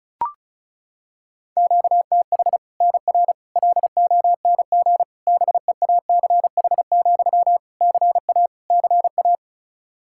Morse code, heard as a keyed tone, about 34 wpm, 700 Hz. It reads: QTH NR LONG BEACH, CA CA
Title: K1USN SST QSO Practice 35wpm